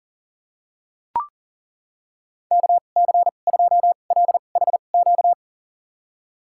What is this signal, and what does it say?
Morse code, heard as a keyed tone, about 40 wpm, spaced as for 29 wpm, 700 Hz. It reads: KC2LHQ
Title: USA Call Signs - 6 Characters - 2x Character Spacing - 40wpm